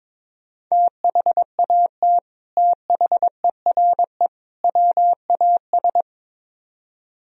Morse code, heard as a keyed tone, about 22 wpm, 700 Hz. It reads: THAT THERE WAS
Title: Speed-Racing - Sets of 3 Words 22wpm